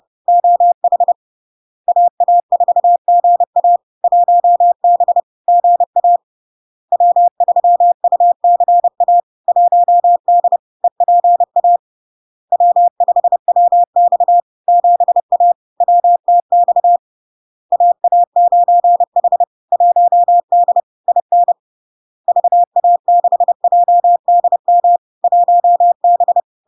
Morse code, heard as a keyed tone, about 30 wpm, 700 Hz. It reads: OH AA4GA 1B GA W3UCA 1D EPA W5WX 7A WTX AA9H 1D IN VA6JDM 1B
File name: LARK-FD-2022-Log-in-CW-30wpm-700-Hz